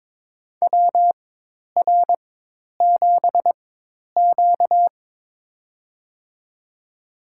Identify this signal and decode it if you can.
Morse code, heard as a keyed tone, about 23 wpm, spaced as for 10 wpm, 700 Hz. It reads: WR7Q
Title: USA Call Signs - 4 Characters - 4x Character Spacing - 22wpm